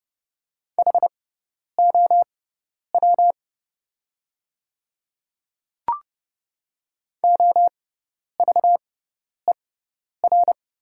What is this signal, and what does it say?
Morse code, heard as a keyed tone, about 31 wpm, spaced as for 10 wpm, 700 Hz. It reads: HOW OVER
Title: 6x Character Spacing - Top 100 Words - 30wpm